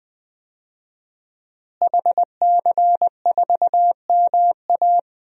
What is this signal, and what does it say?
Morse code, heard as a keyed tone, about 20 wpm, 700 Hz. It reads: HC4MA
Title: DXCC-100 Call Signs - 5 Characters - Repeated 2x - 20wpm